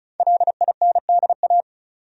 Morse code, heard as a keyed tone, about 36 wpm, 700 Hz. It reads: LINDA
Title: USA Common Names - 35wpm